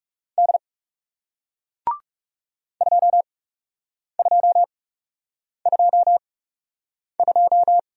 Morse code, heard as a keyed tone, about 40 wpm, spaced as for 8 wpm, 700 Hz. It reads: D 2222